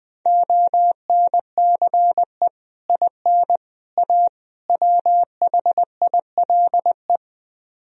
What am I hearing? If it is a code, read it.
Morse code, heard as a keyed tone, about 20 wpm, 700 Hz. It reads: ONCE IN A WHILE
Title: Speed-Racing - Sets of 4 Words 17wpm